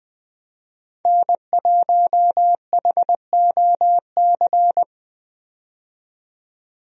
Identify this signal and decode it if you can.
Morse code, heard as a keyed tone, about 20 wpm, 700 Hz. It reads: N1HOC
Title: USA Call Signs - 5 Characters - 20wpm